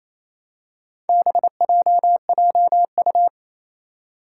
Morse code, heard as a keyed tone, about 28 wpm, 700 Hz. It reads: BJJU
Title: Four Challenging Letter-Number Combinations 28wpm